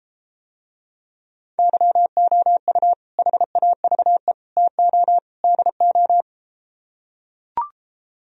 Morse code, heard as a keyed tone, about 33 wpm, 700 Hz. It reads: YOU HAVE TO DO